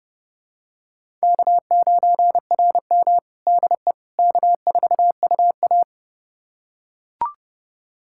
Morse code, heard as a keyed tone, about 29 wpm, 700 Hz. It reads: K9RM DE K4UA
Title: USA Callsign DE Callsign - 4 Characters - 30wpm